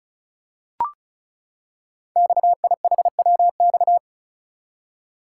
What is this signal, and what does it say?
Morse code, heard as a keyed tone, about 35 wpm, 690 Hz. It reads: XIHWX
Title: Code Groups - 5 Letters - 35wpm